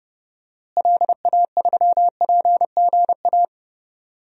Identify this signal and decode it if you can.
Morse code, heard as a keyed tone, about 30 wpm, 700 Hz. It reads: LA3PGA
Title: DXCC-100 Call Signs - 6 Characters - Repeated 3x - 30wpm